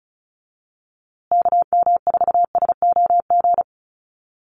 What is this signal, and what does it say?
Morse code, heard as a keyed tone, about 34 wpm, 700 Hz. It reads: KM4SOG
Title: USA Call Signs Repeated 3x 35wpm